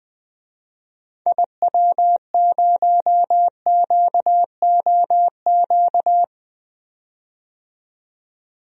Morse code, heard as a keyed tone, about 20 wpm, 700 Hz. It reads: IW0QOQ
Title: DXCC-100 Call Signs - 6 Characters - 20wpm